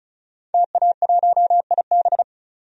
Morse code, heard as a keyed tone, about 34 wpm, 700 Hz. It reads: TA1IB
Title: DXCC-100 Call Signs - 5 Characters - 35wpm